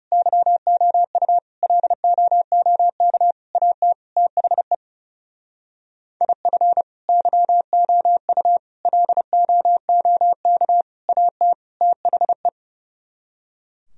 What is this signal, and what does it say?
Morse code, heard as a keyed tone, about 37 wpm, 700 Hz. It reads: YOU LOOK AT THE IF YOU LOOK AT THE
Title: Speed-Racing - Sets of 5 Words 30wpm